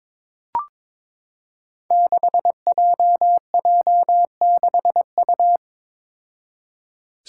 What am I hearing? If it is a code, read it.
Morse code, heard as a keyed tone, about 22 wpm, 700 Hz. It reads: 6JJ6U
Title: Five Challenging Letter-Number Combinations 22wpm